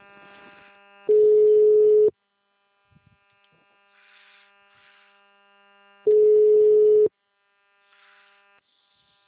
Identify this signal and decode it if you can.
Ringback tone (Europe, 425 Hz).
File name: out- -2007- - - .468